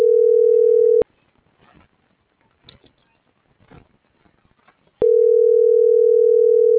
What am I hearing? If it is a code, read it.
Ringback tone (North America), 440 + 480 Hz.